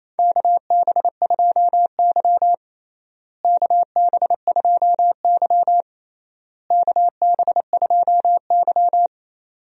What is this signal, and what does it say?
Morse code, heard as a keyed tone, about 28 wpm, 700 Hz. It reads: KB2Y KB2Y KB2Y
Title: USA Call Signs - 4 Characters - Repeated 3x - 28wpm